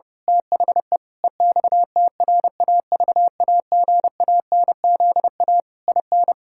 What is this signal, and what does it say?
Morse code, heard as a keyed tone, about 32 wpm, 700 Hz. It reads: THE EXTRAVAGANZA IN
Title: Pangram Challenge v2 - 30wpm